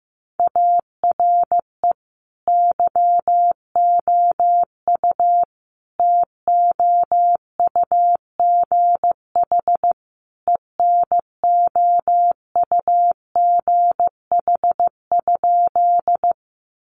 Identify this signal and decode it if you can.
Morse code, heard as a keyed tone, about 14 wpm, 700 Hz. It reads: ARE YOU TOUGH ENOUGH?
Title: Sentences from Top 1200 Words 15wpm